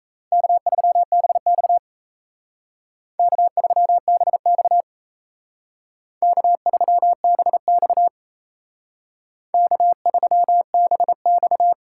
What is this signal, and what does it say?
Morse code, heard as a keyed tone, about 35 wpm, 700 Hz. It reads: K3BX K3BX K3BX K3BX